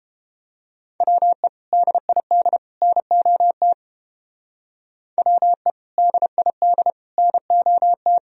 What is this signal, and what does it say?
Morse code, heard as a keyed tone, about 29 wpm, 700 Hz. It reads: WE DID NOT WE DID NOT